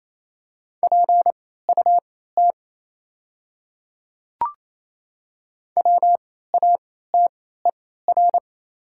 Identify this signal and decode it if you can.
Morse code, heard as a keyed tone, about 28 wpm, spaced as for 16 wpm, 700 Hz. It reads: PUT WATER
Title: 3x Character Spacing - Top 200 Words - 28wpm